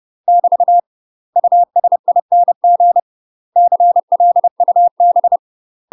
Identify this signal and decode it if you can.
Morse code, heard as a keyed tone, about 30 wpm, 700 Hz. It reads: <BT> USING CLUB
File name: UK MASTER QSOs TRAINING / 30WPM / UK MASTER QSOs 0050